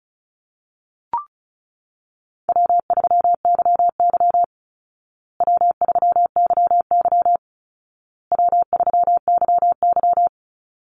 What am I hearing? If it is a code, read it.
Morse code, heard as a keyed tone, about 34 wpm, 700 Hz. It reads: W3YY W3YY W3YY
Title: Call Signs Repeated 2x - 35wpm